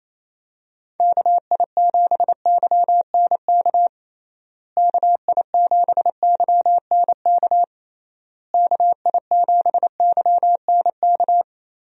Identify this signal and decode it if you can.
Morse code, heard as a keyed tone, about 28 wpm, 700 Hz. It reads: KI7YNK KI7YNK KI7YNK
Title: USA Call Signs - 6 Characters - Repeated 3x - 28wpm